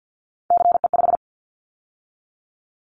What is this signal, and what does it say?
Morse code, heard as a keyed tone, about 50 wpm, 700 Hz. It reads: CE5
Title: Callsign Prefixes for Top 100 DXCC - 50wpm